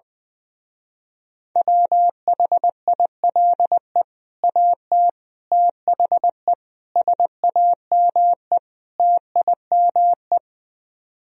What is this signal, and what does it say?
Morse code, heard as a keyed tone, about 20 wpm, 700 Hz. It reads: WHILE AT THE SAME TIME
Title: Speed-Racing - Sets of 5 Words 20wpm